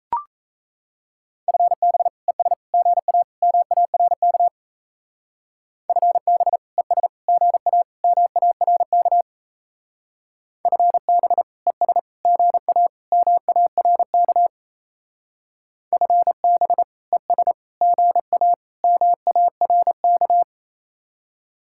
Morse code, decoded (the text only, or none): FB ES GA MARK FB ES GA MARK FB ES GA MARK FB ES GA MARK